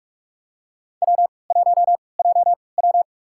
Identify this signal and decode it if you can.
Morse code, heard as a keyed tone, about 46 wpm, spaced as for 26 wpm, 700 Hz. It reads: W1JW
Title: USA Call Signs - 3x Character Spacing - 45wpm